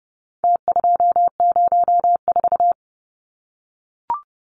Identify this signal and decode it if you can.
Morse code, heard as a keyed tone, about 30 wpm, 700 Hz. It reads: T204
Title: Numbers - Contest Style - 4-Digit 30wpm